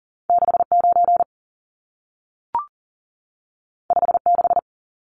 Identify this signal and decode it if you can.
Morse code, heard as a keyed tone, about 40 wpm, 700 Hz. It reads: 69 56
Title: Numbers - 2-Digit 40wpm